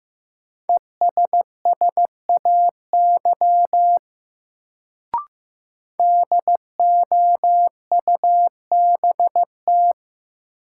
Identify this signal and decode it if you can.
Morse code, heard as a keyed tone, about 15 wpm, 700 Hz. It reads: ESSAY DOUBT
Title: Sentences from Top 1500 Words - New Word Drill Alt - 15wpm